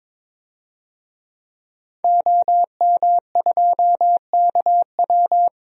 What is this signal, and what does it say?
Morse code, heard as a keyed tone, about 22 wpm, 700 Hz. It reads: OM2KW
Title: DXCC-100 Call Signs - 5 Characters - 22wpm